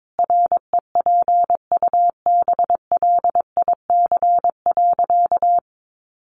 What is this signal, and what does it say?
Morse code, heard as a keyed tone, about 22 wpm, 700 Hz. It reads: REPUBLIC.
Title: Sentences from Top 1500 Words 22wpm